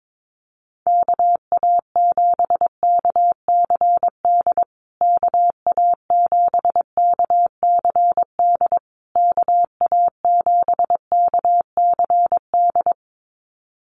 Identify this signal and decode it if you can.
Morse code, heard as a keyed tone, about 22 wpm, 700 Hz. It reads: KA7KCD KA7KCD KA7KCD
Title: USA Call Signs Repeated 3x 22wpm